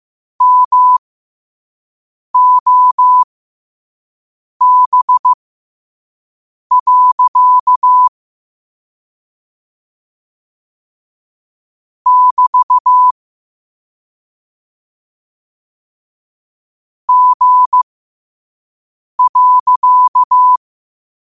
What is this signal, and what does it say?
Morse code, heard as a keyed tone, about 15 wpm, spaced as for 5 wpm, 1000 Hz. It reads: MOB. <BT> G.
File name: qotdfiles / 5wpm